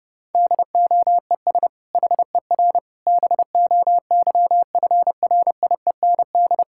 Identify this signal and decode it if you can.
Morse code, heard as a keyed tone, about 32 wpm, 700 Hz. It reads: DOES HER BOYFRIEND
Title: Sentences from Top 1000 Words - Review Part 5 - 30wpm